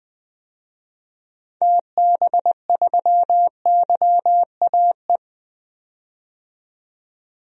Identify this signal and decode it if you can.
Morse code, heard as a keyed tone, about 20 wpm, 700 Hz. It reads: TB3YAE
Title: DXCC-100 Call Signs - 6 Characters - 20wpm